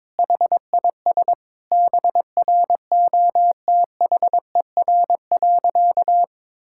Morse code, heard as a keyed tone, about 22 wpm, 700 Hz. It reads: HIS BROTHER.
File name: Difficult Sentences with Hints - 22wpm